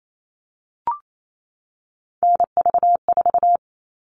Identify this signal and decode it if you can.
Morse code, heard as a keyed tone, about 28 wpm, 700 Hz. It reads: NV4
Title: All Cut Numbers - 3-Digit 28wpm